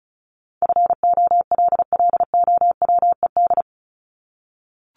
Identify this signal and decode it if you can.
Morse code, heard as a keyed tone, about 34 wpm, 700 Hz. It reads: FOLLOWED
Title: ED Suffix - Top 100 Words with Base Words 35wpm